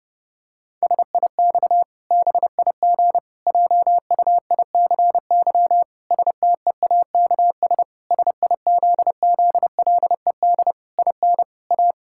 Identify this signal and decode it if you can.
Morse code, heard as a keyed tone, about 30 wpm, 700 Hz. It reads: SIX BIG JUICY STEAKS SIZZLED IN A
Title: Pangram Challenge v2 - 30wpm